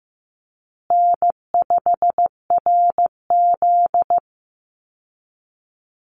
Morse code, heard as a keyed tone, about 14 wpm, 700 Hz. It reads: N5RZ
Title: Easy Call Signs Repeated 3x - 15wpm